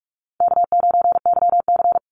Morse code, heard as a keyed tone, about 44 wpm, 700 Hz. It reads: K9YC
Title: Easy Call Signs Repeated 2x - 45wpm